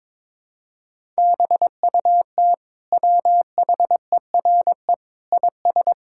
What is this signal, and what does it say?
Morse code, heard as a keyed tone, about 22 wpm, 700 Hz. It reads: BUT WHERE IS
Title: Sentences from Top 1000 Words - Review Part 5 - 22wpm